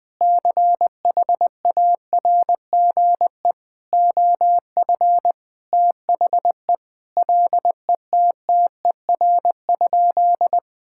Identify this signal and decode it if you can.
Morse code, heard as a keyed tone, about 21 wpm, 700 Hz. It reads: CHARGE OF THE LETTER?
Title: Sentences from Top 1000 Words - Review Part 6 - 20wpm